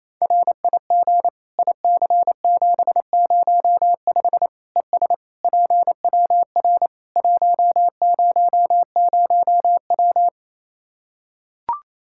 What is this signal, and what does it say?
Morse code, heard as a keyed tone, about 28 wpm, 700 Hz. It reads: RIG IC705 ES PWR 100W
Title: Practice QSO - Exchange 2 - Info - 28wpm